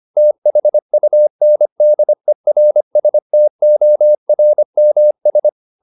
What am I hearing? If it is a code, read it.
Morse code, heard as a keyed tone, about 25 wpm, 600 Hz. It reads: THUNDERSTORMS